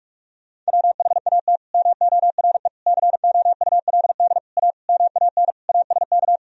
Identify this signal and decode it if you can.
Morse code, heard as a keyed tone, about 47 wpm, 700 Hz. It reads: WHAT MORE COULD A MAN ASK
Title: Sentences from Top 200 Words and Abbreviations 45wpm